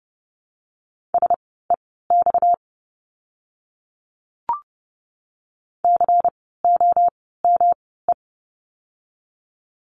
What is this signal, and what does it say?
Morse code, heard as a keyed tone, about 30 wpm, spaced as for 17 wpm, 700 Hz. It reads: SEX COME
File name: course / 30:10wpm / 230 Lesson 37f - Letter x - Words 30wpm